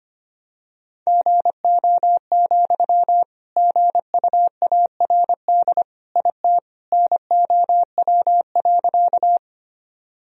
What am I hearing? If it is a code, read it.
Morse code, heard as a keyed tone, about 25 wpm, 700 Hz. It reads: GO, GUARD IT NOW.
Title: Sentences from Top 1000 Words - Review Part 4 - 25wpm